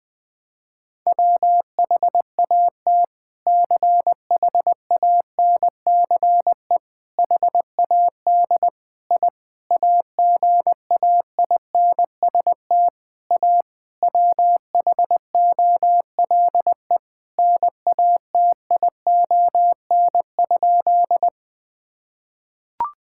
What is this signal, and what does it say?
Morse code, heard as a keyed tone, about 21 wpm, 700 Hz. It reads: WHAT CHANCE HAD I AGAINST A WHOLE NATION?
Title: Sentences from Top 1000 Words - Review Part 6 - 20wpm